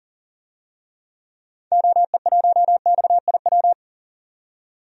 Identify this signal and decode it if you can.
Morse code, heard as a keyed tone, about 40 wpm, 700 Hz. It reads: OE1XIW
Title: DXCC-100 Call Signs - 6 Characters - 40wpm